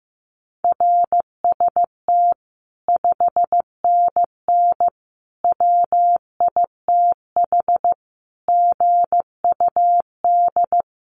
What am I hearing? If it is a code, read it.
Morse code, heard as a keyed tone, about 14 wpm, 700 Hz. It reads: RST 5NN WITH GUD
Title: Common QSO Phrases 15wpm